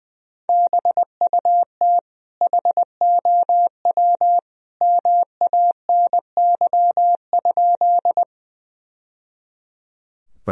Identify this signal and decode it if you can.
Morse code, heard as a keyed tone, about 20 wpm, 700 Hz. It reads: BUT HOW MANY?
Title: Sentences from Top 1000 Words - Review Part 1 - 20wpm